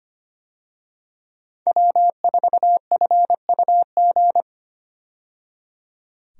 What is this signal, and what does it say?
Morse code, heard as a keyed tone, about 25 wpm, 700 Hz. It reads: W4FUG
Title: USA Call Signs - 5 Characters - 25wpm